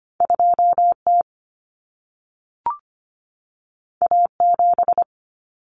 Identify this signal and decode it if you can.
Morse code, heard as a keyed tone, about 25 wpm, 700 Hz. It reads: 2T A7
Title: Common Cut Numbers - 2-Digit 25wpm